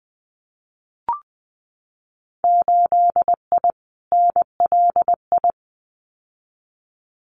Morse code, heard as a keyed tone, about 20 wpm, 700 Hz. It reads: 8I NLI
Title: Winter Field Day Exchanges Encore 20wpm